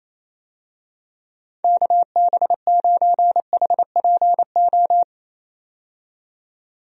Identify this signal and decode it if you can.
Morse code, heard as a keyed tone, about 28 wpm, 700 Hz. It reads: KB9HPO
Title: USA Call Signs - 6 Characters - 28wpm